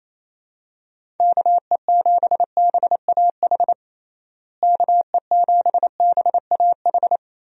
Morse code, heard as a keyed tone, about 28 wpm, 700 Hz. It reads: KE7BAH KE7BAH
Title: USA Call Signs - 6 Characters - Repeated 2x - 28wpm